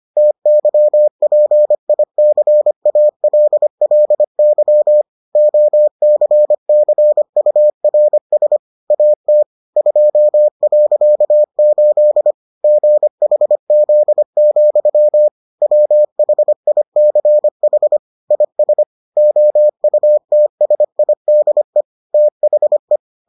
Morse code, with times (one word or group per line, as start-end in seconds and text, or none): 0.2-5.0 TYPICALLY
5.4-8.6 OCCURS
8.9-9.4 AT
9.8-12.3 2.8
12.6-15.3 GHZ,
15.6-18.0 WHICH
18.3-18.8 IS
19.2-21.8 OUTSIDE
22.1-23.0 THE